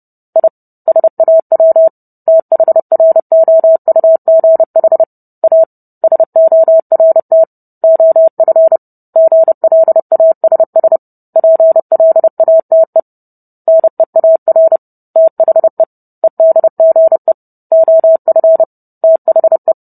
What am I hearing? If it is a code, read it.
Morse code, heard as a keyed tone, about 30 wpm, 650 Hz. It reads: I SAW THROUGH A SORT OF GLASS PLATE NEAR THE EDGE OF THE